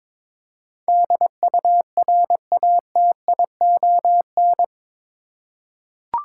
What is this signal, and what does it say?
Morse code, heard as a keyed tone, about 22 wpm, 700 Hz. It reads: DURATION
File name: practice / 22wpm / Eight Letter Words - 22wpm